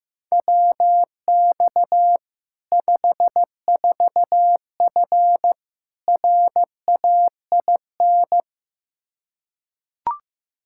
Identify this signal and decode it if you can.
Morse code, heard as a keyed tone, about 15 wpm, 700 Hz. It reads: WX 54F RAIN
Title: Practice QSO - Exchange 2 - Info - 15wpm